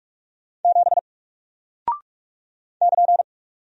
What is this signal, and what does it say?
Morse code, heard as a keyed tone, about 46 wpm, 700 Hz. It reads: 7 <KN>
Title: Single Character (Letter:Number:Punctuation:Prosign) Alt 45wpm